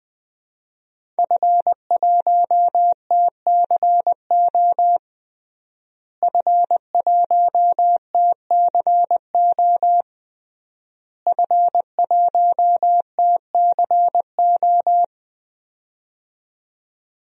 Morse code, heard as a keyed tone, about 20 wpm, 700 Hz. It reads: F1TCO F1TCO F1TCO
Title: DXCC-100 Call Signs - 5 Characters - Repeated 3x - 20wpm